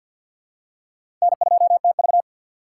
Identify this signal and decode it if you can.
Morse code, heard as a keyed tone, about 51 wpm, 700 Hz. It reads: N2TV
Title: USA Call Signs - 6x Character Spacing - 50wpm